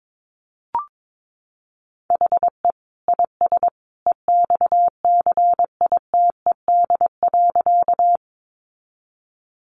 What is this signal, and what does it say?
Morse code, heard as a keyed tone, about 22 wpm, 700 Hz. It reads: HE IS EXCITED.
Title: Additional Easy Phrases 22wpm